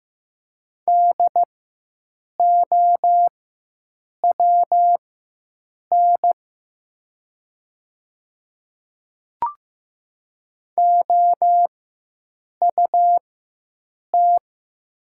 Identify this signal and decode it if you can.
Morse code, heard as a keyed tone, about 15 wpm, spaced as for 7 wpm, 700 Hz. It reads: DOWN OUT
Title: 4x Character Spacing - Top 100 Words - 15wpm